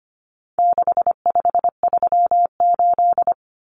Morse code, heard as a keyed tone, about 25 wpm, 700 Hz. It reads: 6538
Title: Common Cut Numbers - 4-Digit 25wpm